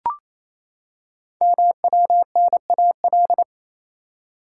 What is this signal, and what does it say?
Morse code, heard as a keyed tone, about 27 wpm, 700 Hz. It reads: MWNAL
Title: Code Groups - 5 Letters - 28wpm